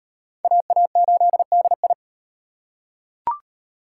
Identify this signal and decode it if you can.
Morse code, heard as a keyed tone, about 38 wpm, 700 Hz. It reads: AA8DI